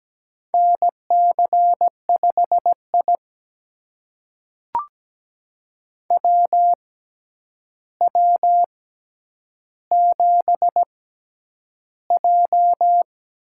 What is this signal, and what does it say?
Morse code, heard as a keyed tone, about 17 wpm, spaced as for 9 wpm, 700 Hz. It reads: NC5I WW7J
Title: USA Call Signs - 4 Characters - 6x Character Spacing - 17wpm